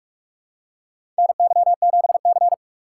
Morse code, heard as a keyed tone, about 45 wpm, 700 Hz. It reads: NY7C
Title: USA Call Signs - 4 Characters - 2x Character Spacing - 45wpm